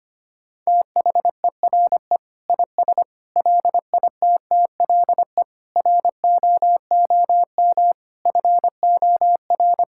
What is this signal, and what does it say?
Morse code, heard as a keyed tone, about 25 wpm, 700 Hz. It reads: THERE IS LITTLE ROOM FOR
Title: Sentences from Top 1000 Words - Review Part 6 - 25wpm